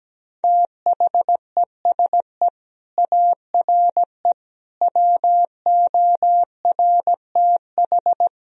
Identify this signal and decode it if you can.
Morse code, heard as a keyed tone, about 17 wpm, 700 Hz. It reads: THESE ARE WORTH
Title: Sentences from Top 1000 Words - Review Part 3 - 17wpm